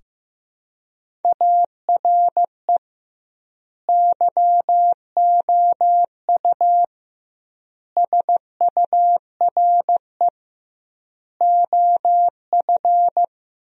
Morse code, heard as a keyed tone, about 15 wpm, 700 Hz. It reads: ARE YOU SURE OF
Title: 2x Word-Spacing - Sentences from Top 300 Words - 15wpm